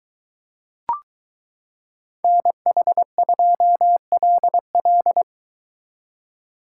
Morse code, heard as a keyed tone, about 22 wpm, 700 Hz. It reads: NH2LL